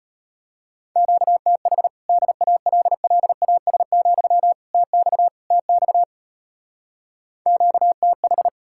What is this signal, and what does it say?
Morse code, heard as a keyed tone, about 33 wpm, 700 Hz. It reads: QTH DALLAS, TX TX QTH